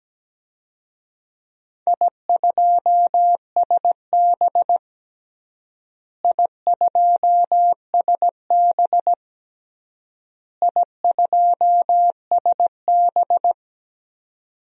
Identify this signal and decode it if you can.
Morse code, heard as a keyed tone, about 17 wpm, 700 Hz. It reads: I2SB I2SB I2SB
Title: DXCC-100 Call Signs - 4 Characters - Repeated 3x - 17wpm